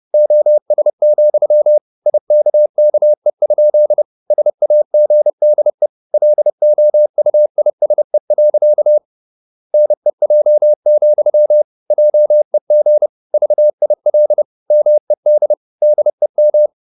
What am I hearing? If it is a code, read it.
Morse code, heard as a keyed tone, about 30 wpm, 600 Hz. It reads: OS, IKKE? SAGDE LOUISE. NEJ, JEG VIL MED DEM
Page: Ferie 30 wpm